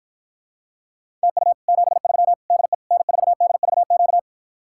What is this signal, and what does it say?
Morse code, heard as a keyed tone, about 54 wpm, 700 Hz. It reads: TU 73 DE N4DVQ